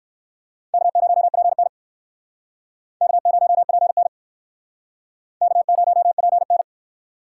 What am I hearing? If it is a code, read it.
Morse code, heard as a keyed tone, about 61 wpm, 700 Hz. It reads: K0PN K0PN K0PN